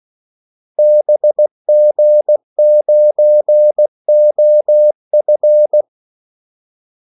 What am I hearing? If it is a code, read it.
Morse code, heard as a keyed tone, about 16 wpm, 600 Hz. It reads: BG9OF
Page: Kaldesignaler 16 wpm